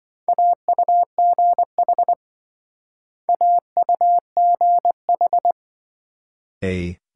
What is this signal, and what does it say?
Morse code, heard as a keyed tone, about 26 wpm, 700 Hz. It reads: AUGH AUGH